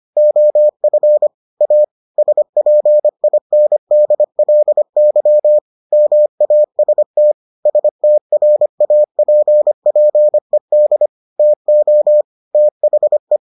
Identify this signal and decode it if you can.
Morse code, heard as a keyed tone, about 25 wpm, 600 Hz. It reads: OF A SPINDLY MAST STRAPPED TO THE